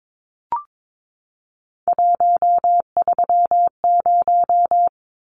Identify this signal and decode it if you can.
Morse code, heard as a keyed tone, about 22 wpm, 700 Hz. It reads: 130
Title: Numbers - 3-Digit 22wpm